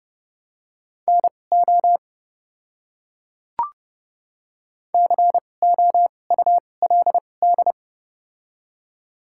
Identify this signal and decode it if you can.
Morse code, heard as a keyed tone, about 30 wpm, spaced as for 22 wpm, 690 Hz. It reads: NO COULD
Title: 2x Character Spacing - Top 100 Words - 30wpm